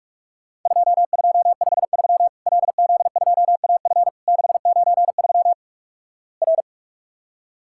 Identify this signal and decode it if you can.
Morse code, heard as a keyed tone, about 46 wpm, 700 Hz. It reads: 2253 LZ2AF 693 R
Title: Worked All Europe DX Contest - QTC Exchange - 45wpm